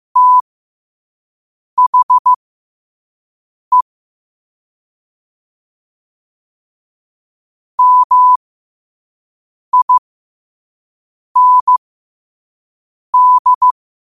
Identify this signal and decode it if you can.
Morse code, heard as a keyed tone, about 15 wpm, spaced as for 5 wpm, 1000 Hz. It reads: THE MIND
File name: qotdfiles / 5wpm